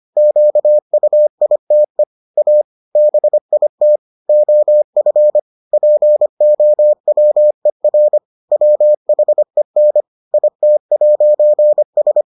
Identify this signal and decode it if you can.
Morse code, heard as a keyed tone, about 25 wpm, 600 Hz. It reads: QUITE A BIT OF POWER WHEN IT'S